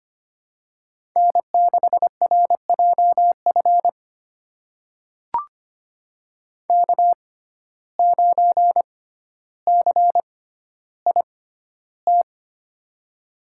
Morse code, heard as a keyed tone, about 25 wpm, spaced as for 13 wpm, 700 Hz. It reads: N6RJF K9CIT
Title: USA Call Signs - 5 Characters - 6x Character Spacing - 25wpm